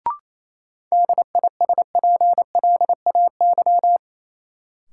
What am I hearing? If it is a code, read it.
Morse code, heard as a keyed tone, about 28 wpm, 700 Hz. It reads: DISPLAY
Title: Sentences from Top 1200 Words - New Word Drill Alt - 28wpm